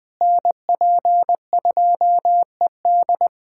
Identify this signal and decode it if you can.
Morse code, heard as a keyed tone, about 20 wpm, 700 Hz. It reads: NP2ED
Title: USA Callsign DE Callsign - 5 Characters - 20wpm